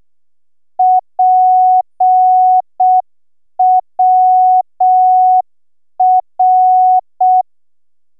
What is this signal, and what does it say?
Morse code, heard as a keyed tone, about 6 wpm, 740 Hz. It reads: PWR